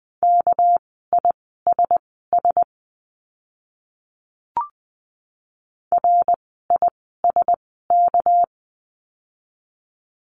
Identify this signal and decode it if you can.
Morse code, heard as a keyed tone, about 20 wpm, spaced as for 14 wpm, 700 Hz. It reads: KISS RISK
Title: 199 Lesson 32f - Letter k - Words 20wpm